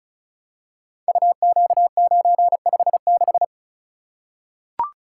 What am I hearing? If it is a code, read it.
Morse code, heard as a keyed tone, about 36 wpm, 700 Hz. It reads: UQ956
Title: Code Groups - 5 Letter:Numbers - 35wpm